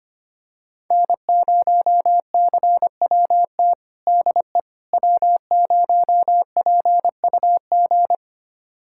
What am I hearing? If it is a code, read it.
Morse code, heard as a keyed tone, about 25 wpm, 700 Hz. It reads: N0CWT DE W0PUG
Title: USA Callsign DE Callsign - 5 Characters - 25wpm